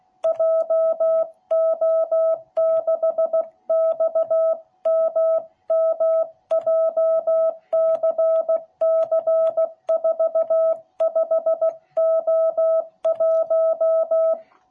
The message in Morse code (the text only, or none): JO6XMM JCC45O1